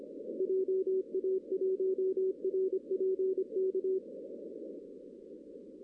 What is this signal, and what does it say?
Morse code, heard as a keyed tone, about 26 wpm, 380 Hz. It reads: JA1RPK
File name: 2016 arrl dx cw / 2016-ARRL-DX-CW-K5TR- - 0-25